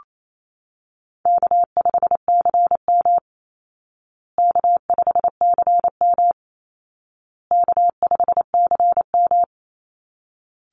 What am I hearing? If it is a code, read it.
Morse code, heard as a keyed tone, about 28 wpm, 700 Hz. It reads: K5CM K5CM K5CM
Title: Easy Call Signs Repeated 3x - 28wpm